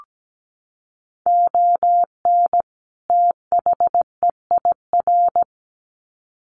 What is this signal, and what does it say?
Morse code, heard as a keyed tone, about 17 wpm, 700 Hz. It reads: ON THEIR
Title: Sets of 2 Words 17wpm